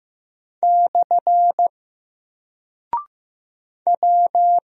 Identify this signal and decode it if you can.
Morse code, heard as a keyed tone, about 15 wpm, 700 Hz. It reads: / W
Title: Single Character (Letter:Number:Punctuation:Prosign) Alt 15wpm